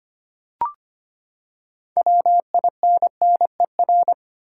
Morse code, heard as a keyed tone, about 25 wpm, 700 Hz. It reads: WINNER
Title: Six Letter Words - 25wpm